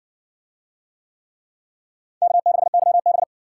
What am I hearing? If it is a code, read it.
Morse code, heard as a keyed tone, about 56 wpm, 700 Hz. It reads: K6YB